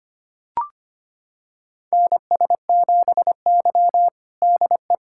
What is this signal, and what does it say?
Morse code, heard as a keyed tone, about 25 wpm, 700 Hz. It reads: NS7Y DE
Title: USA Callsign DE Callsign - 4 Characters - 25wpm